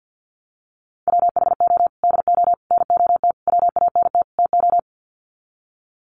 Morse code, heard as a keyed tone, about 50 wpm, 700 Hz. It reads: WHO DO NOT WANT TO
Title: Sets of 5 Words Encore2 50wpm